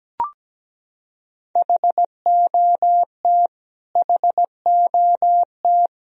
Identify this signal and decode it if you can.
Morse code, heard as a keyed tone, about 17 wpm, 700 Hz. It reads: HOT HOT
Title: Top 100 Words Repeated 17wpm